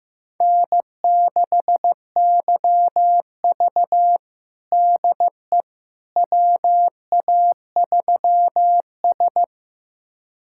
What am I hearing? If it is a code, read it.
Morse code, heard as a keyed tone, about 15 wpm, 700 Hz. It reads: N6YV DE WA3S
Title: USA Callsign DE Callsign - 4 Characters - 15wpm